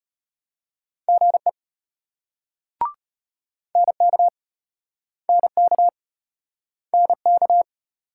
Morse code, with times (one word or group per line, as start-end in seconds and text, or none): 1.1-1.5 GE
3.8-4.3 NK
5.3-5.9 NK
6.9-7.6 NK